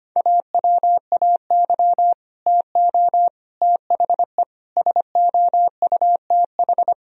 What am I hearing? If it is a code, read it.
Morse code, heard as a keyed tone, about 25 wpm, 700 Hz. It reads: AWAY TO THE SOUTH
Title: Sentences from Top 1000 Words - Review Part 2 - 25wpm